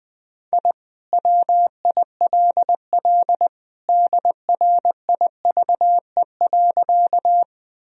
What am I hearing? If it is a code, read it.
Morse code, heard as a keyed tone, about 20 wpm, 700 Hz. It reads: I WILL DRIVE.
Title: Sentences from Top 1000 Words - Review Part 2 - 20wpm